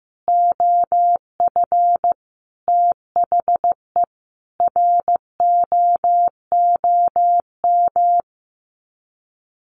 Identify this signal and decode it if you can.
Morse code, heard as a keyed tone, about 15 wpm, 700 Hz. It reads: OF THE ROOM
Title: Sets of 5 Words Encore2 15wpm